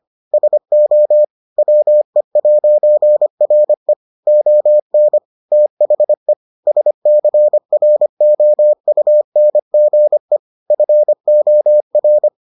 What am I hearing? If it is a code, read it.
Morse code, heard as a keyed tone, about 25 wpm, 600 Hz. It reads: SO WE'RE ON THE SCROUNGE FOR